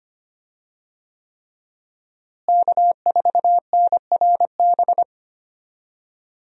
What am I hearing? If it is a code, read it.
Morse code, heard as a keyed tone, about 25 wpm, 700 Hz. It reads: K4NRB
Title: USA Call Signs - 5 Characters - Repeated 3x - 25wpm